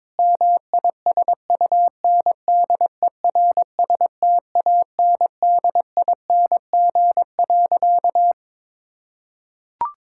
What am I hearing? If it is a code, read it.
Morse code, heard as a keyed tone, about 22 wpm, 700 Hz. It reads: MISUNDERSTANDING.
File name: practice / 22wpm / Sentences with 4-Word Phrases - 22wpm